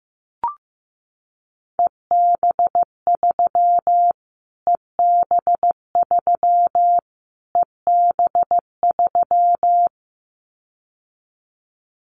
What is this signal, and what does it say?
Morse code, heard as a keyed tone, about 15 wpm, 700 Hz. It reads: EB3 EB3 EB3
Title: Callsign Prefixes for Top 25 DXCC - Repeated 3x - 15wpm